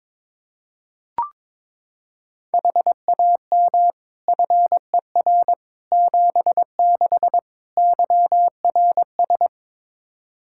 Morse code, heard as a keyed tone, about 22 wpm, 700 Hz. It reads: HAM FER 76 YRS
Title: K1USN SST QSO Practice 22wpm